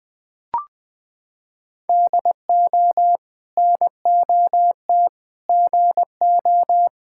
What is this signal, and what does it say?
Morse code, heard as a keyed tone, about 21 wpm, 700 Hz. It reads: DO NOT GO
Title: Sentences from Top 1000 Words - Review Part 6 - 20wpm